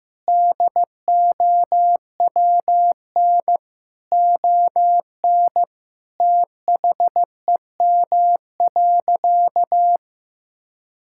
Morse code, heard as a keyed tone, about 15 wpm, 700 Hz. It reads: DOWN ON THEM.
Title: Sentences from Top 1600 Words 15wpm